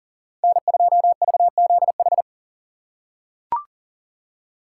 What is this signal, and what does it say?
Morse code, heard as a keyed tone, about 40 wpm, 700 Hz. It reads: N2VZH
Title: USA Call Signs - 6x Character Spacing - 40wpm